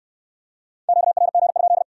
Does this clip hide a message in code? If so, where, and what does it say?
Yes: Morse code from 0.9 to 1.8 s, heard as a keyed tone, about 65 wpm, 700 Hz. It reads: QRG?